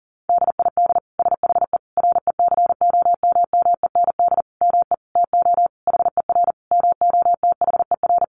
Morse code, heard as a keyed tone, about 40 wpm, 700 Hz. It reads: DID SHE RECOMMEND ME TO HER MOTHER
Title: Sentences from Top 1200 Words 40wpm